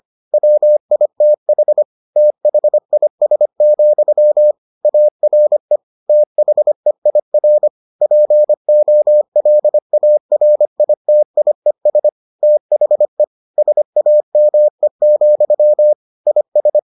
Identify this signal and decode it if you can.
Morse code, heard as a keyed tone, about 25 wpm, 600 Hz. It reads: WITH THIS, ARE THEIR POLARITIES THE SAME, IS